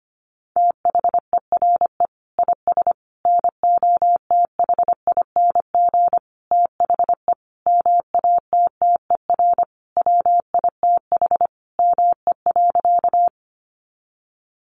Morse code, heard as a keyed tone, about 25 wpm, 700 Hz. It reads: THERE IS NOTHING THE MATTER WITH ME.
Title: Sentences from Top 700 Words 25wpm